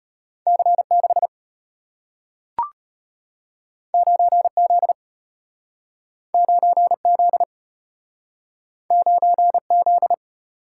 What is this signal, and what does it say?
Morse code, heard as a keyed tone, about 33 wpm, 690 Hz. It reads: C6 9Z 9Z 9Z